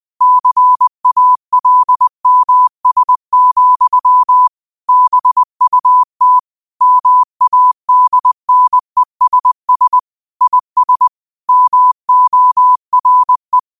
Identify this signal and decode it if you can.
Morse code, heard as a keyed tone, about 20 wpm, 1000 Hz. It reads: CALMS, BUT MADNESS IS MORE
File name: qotdfiles / 20wpm